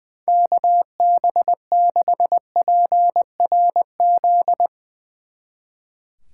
Morse code, heard as a keyed tone, about 20 wpm, 700 Hz. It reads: KB6PRZ
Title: Speed Racing - USA 2x3 Call Signs 20wpm